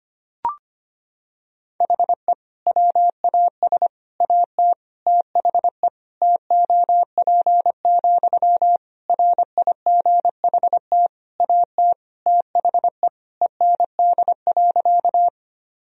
Morse code, heard as a keyed tone, about 25 wpm, 700 Hz. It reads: HE WAS AT THE TOP, RIGHT AT THE END.
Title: Sentences from Top 1000 Words - Review Part 2 - 25wpm